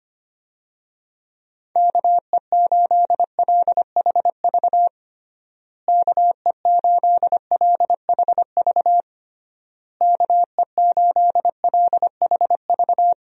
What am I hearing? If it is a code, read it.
Morse code, heard as a keyed tone, about 25 wpm, 700 Hz. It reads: KE8LHV KE8LHV KE8LHV
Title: USA Call Signs - 6 Characters - Repeated 3x - 25wpm